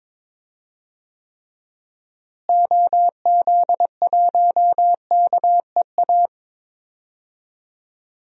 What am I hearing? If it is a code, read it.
Morse code, heard as a keyed tone, about 21 wpm, 700 Hz. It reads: OZ1KEA
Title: DXCC-100 Call Signs - 6 Characters - Repeated 2x - 22wpm